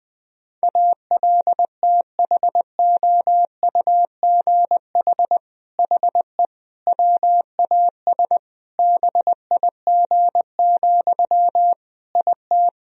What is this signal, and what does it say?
Morse code, heard as a keyed tone, about 20 wpm, 700 Hz. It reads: ALTHOUGH HE WAS BIG, IT
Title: Sentences from Top 1000 Words - Review Part 3 - 20wpm